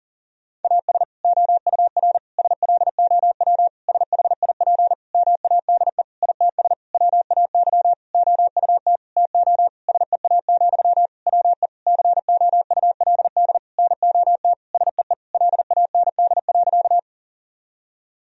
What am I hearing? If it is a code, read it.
Morse code, heard as a keyed tone, about 37 wpm, 700 Hz. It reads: AS OUR SLOW SHIP MADE ITS WAY OUT TO SEA, WE COULD NOT SEE LAND.
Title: Sentences from Top 1000 Words - Review Part 2 - 40wpm